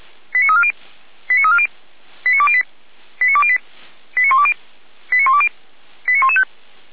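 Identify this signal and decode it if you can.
Five-tone selective call (ZVEI), heard as quick runs of five tones, tones of about 70 ms. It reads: 89340 89340 89208 89208 89120 89120 89115
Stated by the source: Five-tone ZVEI selective call received on SDR.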